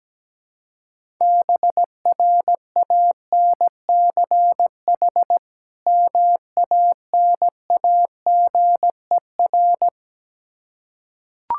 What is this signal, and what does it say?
Morse code, heard as a keyed tone, about 17 wpm, 700 Hz. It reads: BRANCH MANAGER
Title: Practice QSO - Exchange 3 - More Info - 17wpm